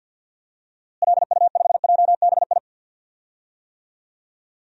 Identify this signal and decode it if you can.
Morse code, heard as a keyed tone, about 47 wpm, 700 Hz. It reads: LU5JBI
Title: DXCC-100 Call Signs - 6 Characters - Repeated 3x - 50wpm